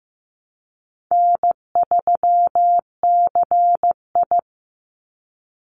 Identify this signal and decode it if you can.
Morse code, heard as a keyed tone, about 15 wpm, 700 Hz. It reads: N3CI
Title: Call Signs Repeated 2x - 15wpm